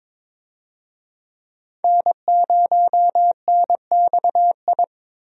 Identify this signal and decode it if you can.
Morse code, heard as a keyed tone, about 21 wpm, 700 Hz. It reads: N0NXI
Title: USA Call Signs - 5 Characters - Repeated 3x - 22wpm